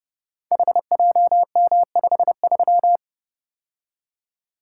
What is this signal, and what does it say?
Morse code, heard as a keyed tone, about 30 wpm, 700 Hz. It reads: HJM53
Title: Code Groups - 5 Letter:Numbers - 30wpm